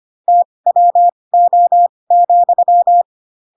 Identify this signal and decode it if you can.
Morse code, heard as a keyed tone, about 25 wpm, spaced as for 20 wpm, 700 Hz. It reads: TWO,
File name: Pinocchio / Pinocchio ch6 15 25 ik1hge